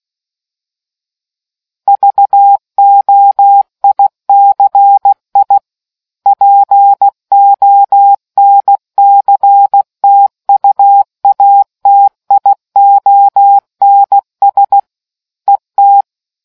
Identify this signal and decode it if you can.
Morse code, heard as a keyed tone, about 16 wpm, 800 Hz. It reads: VOICI PONCTUATIONS ET